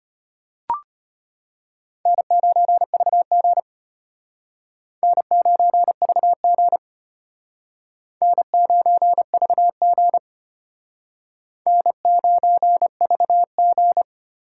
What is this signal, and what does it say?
Morse code, heard as a keyed tone, about 31 wpm, 700 Hz. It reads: N9VG N9VG N9VG N9VG
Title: Speed Racing - USA 1x2 Call Signs 25wpm